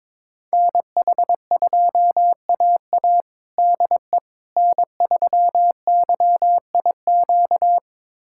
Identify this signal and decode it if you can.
Morse code, heard as a keyed tone, about 22 wpm, 700 Hz. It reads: NH2AA DE N3YIQ
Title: USA Callsign DE Callsign - 5 Characters - 22wpm